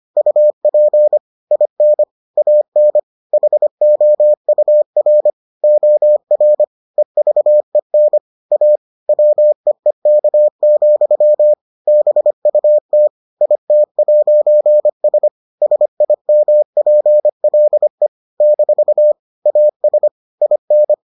Morse code, heard as a keyed tone, about 25 wpm, 600 Hz. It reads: UP IN AN HOUR OR EVEN A WEEK, BUT IT'S SIMPLE - AS IN